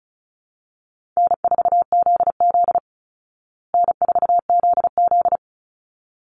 Morse code, heard as a keyed tone, about 34 wpm, 700 Hz. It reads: N4ZZ N4ZZ
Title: Call Signs Repeated 3x - 35wpm